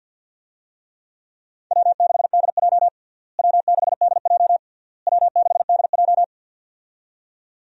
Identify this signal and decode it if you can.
Morse code, heard as a keyed tone, about 51 wpm, 700 Hz. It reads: W6DJ W6DJ W6DJ
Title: USA Call Signs - 4 Characters - Repeated 3x - 50wpm